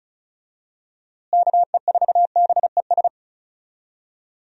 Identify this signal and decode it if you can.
Morse code, heard as a keyed tone, about 34 wpm, 700 Hz. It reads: KE4BES
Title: USA Call Signs - 6 Characters - Repeated 2x - 35wpm